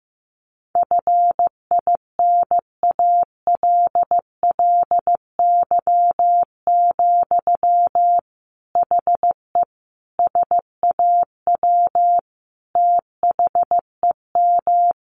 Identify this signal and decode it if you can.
Morse code, heard as a keyed tone, about 14 wpm, 700 Hz. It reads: FINALLY, HE SAW THEM
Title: Sentences from Top 1300 Words 15wpm